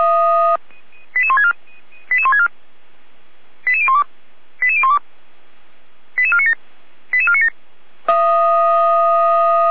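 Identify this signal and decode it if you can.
Five-tone selective call (ZVEI), heard as quick runs of five tones, tones of about 70 ms. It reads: 88164 88164 88012 88012 88497 88497
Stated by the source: Five-tone ZVEI selective call received by SDR, from the feuerwehr-orlamuende.org fire brigade audio archive.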